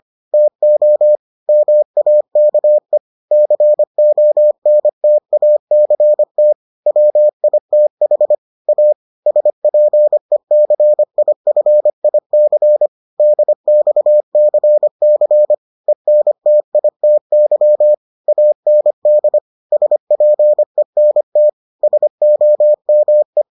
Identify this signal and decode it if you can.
Morse code, heard as a keyed tone, about 25 wpm, 600 Hz. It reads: TO MAKE CONTACT WITH A SPECIFIC DXCC ENTITY AND SPENT SOME